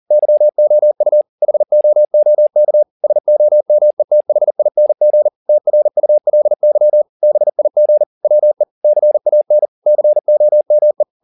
Morse code, heard as a keyed tone, about 40 wpm, 600 Hz. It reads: YOU HOOK SOMETHING TRULY BIG WE CAN COME